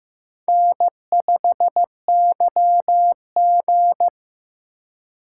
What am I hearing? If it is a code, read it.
Morse code, heard as a keyed tone, about 15 wpm, 700 Hz. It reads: N5YG
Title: USA Call Signs - 4 Characters - 3x Character Spacing - 15wpm